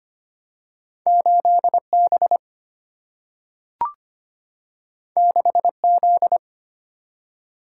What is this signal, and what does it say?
Morse code, heard as a keyed tone, about 25 wpm, 700 Hz. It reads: 8B 6Z
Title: Single-Letter Single-Number 25wpm